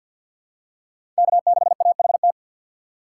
Morse code, heard as a keyed tone, about 48 wpm, 700 Hz. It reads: K6AHT
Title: USA Call Signs - 3x Character Spacing - 50wpm